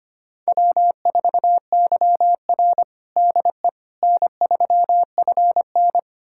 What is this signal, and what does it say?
Morse code, heard as a keyed tone, about 25 wpm, 700 Hz. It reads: W4YR DE N3FN
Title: USA Callsign DE Callsign - 4 Characters - 25wpm